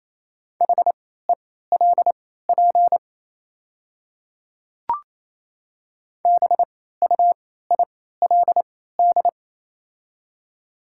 Morse code, heard as a keyed tone, about 28 wpm, spaced as for 16 wpm, 700 Hz. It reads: HELP BUILD
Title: 3x Character Spacing - Top 200 Words - 28wpm